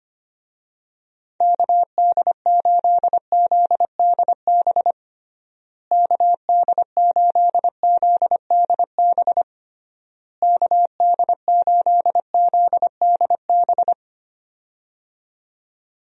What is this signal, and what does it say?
Morse code, heard as a keyed tone, about 25 wpm, 700 Hz. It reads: KD8ZDB KD8ZDB KD8ZDB
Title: USA Call Signs - 6 Characters - Repeated 3x - 25wpm